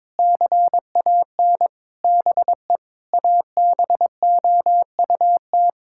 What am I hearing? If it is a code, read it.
Morse code, heard as a keyed tone, about 22 wpm, 700 Hz. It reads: CAN BE ABOUT
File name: Sentences with 9-Word Phrases - 22wpm